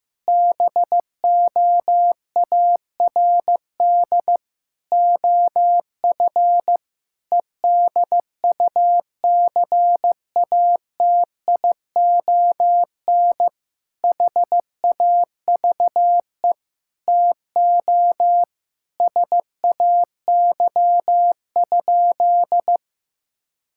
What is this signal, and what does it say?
Morse code, heard as a keyed tone, about 15 wpm, 700 Hz. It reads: BOARD OF EDUCATION HAVE TO SAY?
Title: Sentences from Top 1000 Words - Review Part 6 - 15wpm